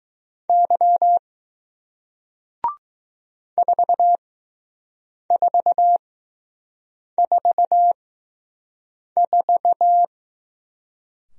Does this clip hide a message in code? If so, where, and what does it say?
Yes: Morse code from 0.5 to 10.0 s, heard as a keyed tone, about 20 wpm, spaced as for 6 wpm, 700 Hz. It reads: Y 4444